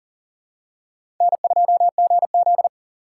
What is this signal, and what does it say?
Morse code, heard as a keyed tone, about 40 wpm, 700 Hz. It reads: N2GZ
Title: USA Call Signs - 4 Characters - 4x Character Spacing - 40wpm